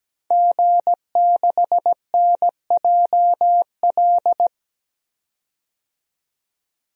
Morse code, heard as a keyed tone, about 17 wpm, 700 Hz. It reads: G6NJL
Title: DXCC-100 Call Signs - 5 Characters - 17wpm